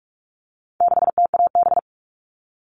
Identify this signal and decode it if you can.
Morse code, heard as a keyed tone, about 44 wpm, 700 Hz. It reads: 6TAB
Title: All Cut Numbers - 4-Digit 45wpm